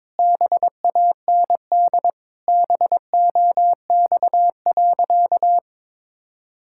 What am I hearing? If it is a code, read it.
Morse code, heard as a keyed tone, about 22 wpm, 700 Hz. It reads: BAND BOX.
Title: Sentences from Top 1600 Words 22wpm